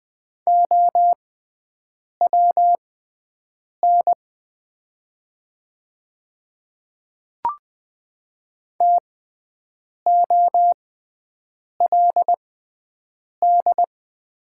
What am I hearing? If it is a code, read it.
Morse code, heard as a keyed tone, about 20 wpm, spaced as for 7 wpm, 700 Hz. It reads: OWN TOLD
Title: 6x Character Spacing - Top 500 Words - 20wpm